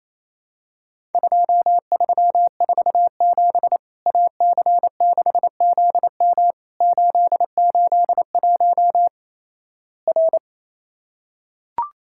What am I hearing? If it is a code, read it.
Morse code, heard as a keyed tone, about 28 wpm, 700 Hz. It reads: 2347 AC6ZM 881 R
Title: Worked All Europe DX Contest - QTC Exchange - 28wpm